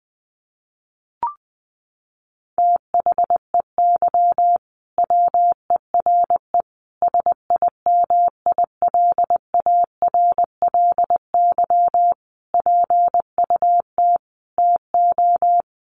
Morse code, heard as a keyed tone, about 19 wpm, 700 Hz. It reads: THEY WERE SIMILARLY PUT TO
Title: Sentences from Top 1400 Words 20wpm